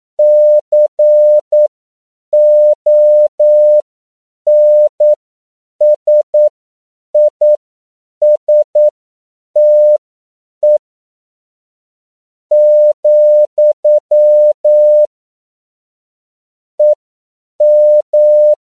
Morse code, heard as a keyed tone, about 9 wpm, spaced as for 7 wpm, 600 Hz. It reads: CONSISTE, EM
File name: CW / 10PPM